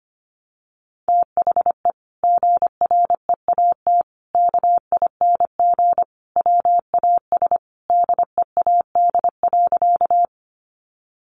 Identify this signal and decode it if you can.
Morse code, heard as a keyed tone, about 25 wpm, 700 Hz. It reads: THE GREAT KING WAS DEAD.
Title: Sentences from Top 1500 Words 25wpm